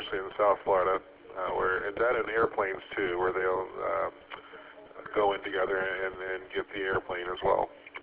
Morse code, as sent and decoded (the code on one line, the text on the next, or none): none